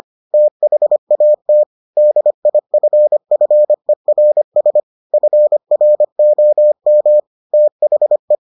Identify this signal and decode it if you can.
Morse code, heard as a keyed tone, about 25 wpm, 600 Hz. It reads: THAT DIFFERS FROM THE